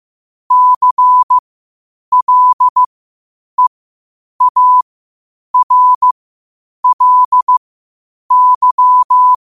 Morse code, heard as a keyed tone, about 15 wpm, spaced as for 8 wpm, 1000 Hz. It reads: CLEARLY
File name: qotdfiles / 8wpm